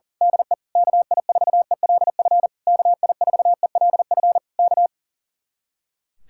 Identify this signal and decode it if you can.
Morse code, heard as a keyed tone, about 38 wpm, 700 Hz. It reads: DE KI4ELF KI4ELF K
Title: Speed-Racing Common QSO Phrases Encore - 40wpm